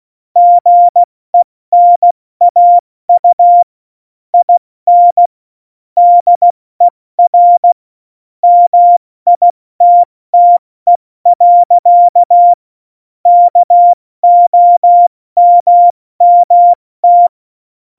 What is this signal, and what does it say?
Morse code, heard as a keyed tone, about 16 wpm, spaced as for 14 wpm, 700 Hz. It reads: GENAU IN DER MITTE. KOMMT
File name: Morsekurs / Witze / Witz 23. April 2019, 70 F80